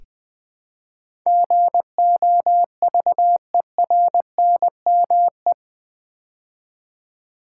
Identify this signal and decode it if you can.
Morse code, heard as a keyed tone, about 20 wpm, 700 Hz. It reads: GOVERNME
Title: Build A Word - 4 Letters At A Time - 10-Letter Words - 20wpm